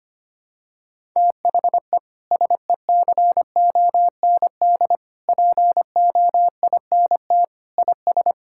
Morse code, heard as a keyed tone, about 25 wpm, 700 Hz. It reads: THE SECOND POINT IS
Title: Sentences from Top 1000 Words - Review Part 5 - 25wpm